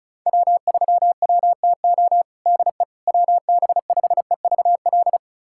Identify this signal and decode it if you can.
Morse code, heard as a keyed tone, about 36 wpm, 690 Hz. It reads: W3WTO DE WB5EVL
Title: USA Callsign DE Callsign 35wpm